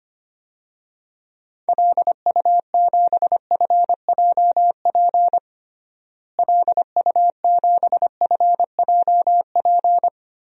Morse code, heard as a keyed tone, about 25 wpm, 700 Hz. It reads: LU7FJP LU7FJP
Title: DXCC-100 Call Signs - 6 Characters - Repeated 3x - 25wpm